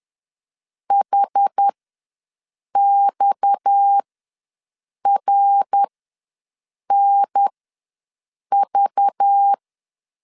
Morse code, heard as a keyed tone, about 10 wpm, spaced as for 6 wpm, 800 Hz. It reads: HXRNV
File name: LEZIONI / 6 PKX0 / 6 PKX ws 10wpm 3706